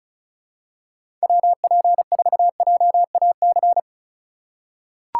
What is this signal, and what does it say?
Morse code, heard as a keyed tone, about 34 wpm, 700 Hz. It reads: WP4JAC
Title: USA Call Signs - 3x Character Spacing - 35wpm